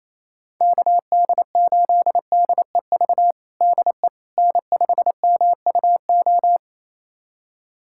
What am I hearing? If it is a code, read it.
Morse code, heard as a keyed tone, about 28 wpm, 700 Hz. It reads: KD8DEV DE N5MUO
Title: USA Callsign DE Callsign 28wpm